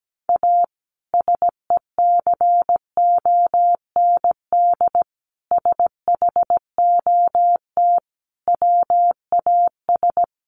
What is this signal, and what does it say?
Morse code, heard as a keyed tone, about 17 wpm, 700 Hz. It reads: A SECOND SHOT WAS
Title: Sentences from Top 1100 Words 17wpm